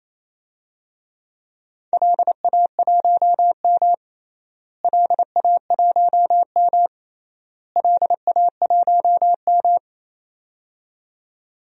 Morse code, heard as a keyed tone, about 28 wpm, 700 Hz. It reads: LA1M LA1M LA1M
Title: DXCC-100 Call Signs - 4 Characters - Repeated 3x - 28wpm